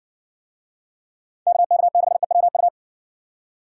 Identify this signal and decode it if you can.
Morse code, heard as a keyed tone, about 60 wpm, 700 Hz. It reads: KK6EWU